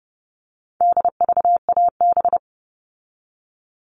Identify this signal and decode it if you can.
Morse code, heard as a keyed tone, about 30 wpm, 700 Hz. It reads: DVAB
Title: All Cut Numbers - 4-Digit 30wpm